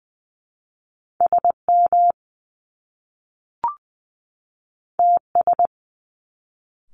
Morse code, heard as a keyed tone, about 20 wpm, 700 Hz. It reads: S M T S
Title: Top Two Letter Combinations in Words 20wpm